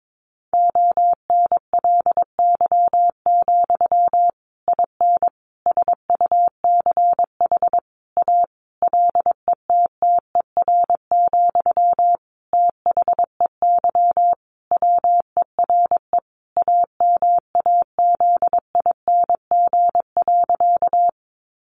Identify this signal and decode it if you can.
Morse code, heard as a keyed tone, about 22 wpm, 700 Hz. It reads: ONLY, IN SUCH A LETTER, THEY WERE AMAZING.
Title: Sentences from Top 1500 Words 22wpm